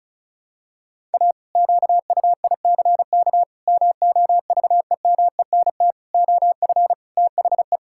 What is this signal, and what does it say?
Morse code, heard as a keyed tone, about 36 wpm, 700 Hz. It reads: A QUICK MOVEMENT OF THE
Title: Pangram Challenge v2 - 35wpm